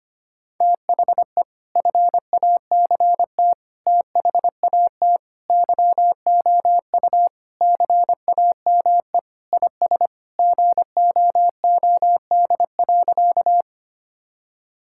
Morse code, heard as a keyed tone, about 25 wpm, 700 Hz. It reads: THE FACT THAT YOU CAME IS GOOD.
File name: practice / 25wpm / Sentences with 3-Word Phrases - 25wpm